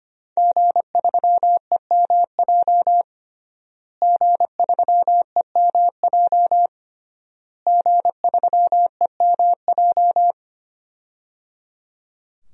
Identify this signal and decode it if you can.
Morse code, heard as a keyed tone, about 25 wpm, 700 Hz. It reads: G3EMJ G3EMJ G3EMJ
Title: DXCC-100 Call Signs - 5 Characters - Repeated 3x - 25wpm